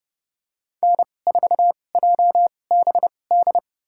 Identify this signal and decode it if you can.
Morse code, heard as a keyed tone, about 30 wpm, spaced as for 22 wpm, 700 Hz. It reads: N4JBD
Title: USA Call Signs - 5 Characters - 2x Character Spacing - 30wpm